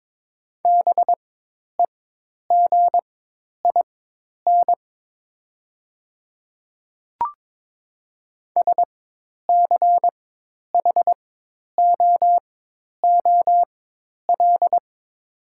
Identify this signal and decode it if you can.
Morse code, heard as a keyed tone, about 22 wpm, spaced as for 10 wpm, 700 Hz. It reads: BEGIN SCHOOL
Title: 4x Character Spacing - Top 300 Words - 22wpm